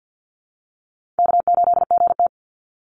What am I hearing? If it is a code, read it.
Morse code, heard as a keyed tone, about 49 wpm, 700 Hz. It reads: K8GT
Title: Easy Call Signs Repeated 2x - 50wpm